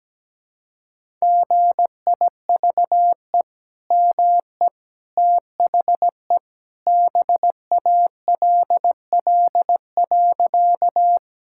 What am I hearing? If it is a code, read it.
Morse code, heard as a keyed tone, about 17 wpm, 700 Hz. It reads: GIVE ME THE BALL.
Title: Sentences from Top 1000 Words - Review Part 6 - 17wpm